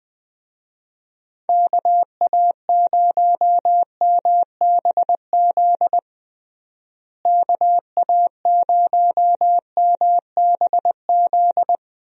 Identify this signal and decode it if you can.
Morse code, heard as a keyed tone, about 20 wpm, 700 Hz. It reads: KA0MBZ KA0MBZ
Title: USA Call Signs - 6 Characters - Repeated 2x - 20wpm